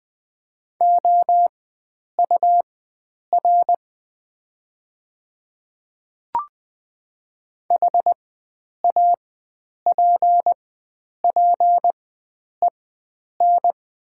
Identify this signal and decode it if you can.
Morse code, heard as a keyed tone, about 20 wpm, spaced as for 9 wpm, 700 Hz. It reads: OUR HAPPEN
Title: 4x Character Spacing - Top 500 Words - 20wpm